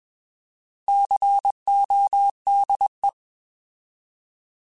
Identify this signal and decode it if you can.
Morse code, heard as a keyed tone, about 21 wpm, 800 Hz. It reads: CODE
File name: Alphabetical / Alphabetical mp3s / Cs @20